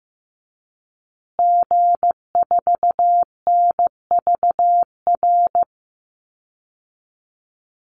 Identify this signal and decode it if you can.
Morse code, heard as a keyed tone, about 15 wpm, 700 Hz. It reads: G4NVR
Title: Call Signs Repeated 3x - 15wpm